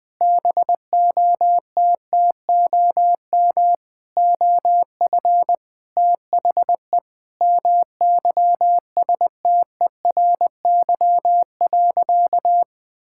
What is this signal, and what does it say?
Morse code, heard as a keyed tone, about 21 wpm, 700 Hz. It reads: BOTTOM OF THE MYSTERY.
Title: Sentences with 5-Word Phrases - 20wpm